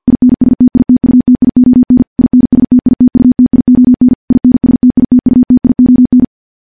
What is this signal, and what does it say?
Morse code, heard as a keyed tone, about 50 wpm, 250 Hz. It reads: INSTITUTION INSTITUTION INSTITUTION
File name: morse-training-50WPM